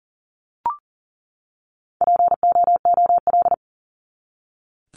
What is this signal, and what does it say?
Morse code, heard as a keyed tone, about 40 wpm, 700 Hz. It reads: POOL
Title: Top 1751-2000 Words 40wpm